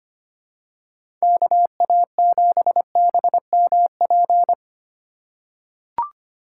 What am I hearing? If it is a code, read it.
Morse code, heard as a keyed tone, about 25 wpm, 700 Hz. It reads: KA7BMP
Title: USA Call Signs - 2x Character Spacing - 25wpm